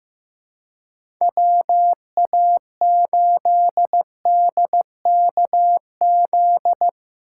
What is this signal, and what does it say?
Morse code, heard as a keyed tone, about 15 wpm, 700 Hz. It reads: WA8DKZ
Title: USA Call Signs - 6 Characters - 4x Character Spacing - 15wpm